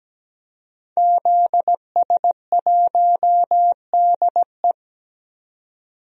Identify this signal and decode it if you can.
Morse code, heard as a keyed tone, about 17 wpm, 700 Hz. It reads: ZS1DE
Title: DXCC-100 Call Signs - 5 Characters - 17wpm